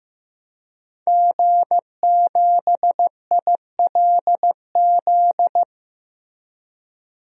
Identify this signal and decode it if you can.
Morse code, heard as a keyed tone, about 15 wpm, 700 Hz. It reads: G7ILZ
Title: DXCC-100 Call Signs - 5 Characters - Repeated 3x - 15wpm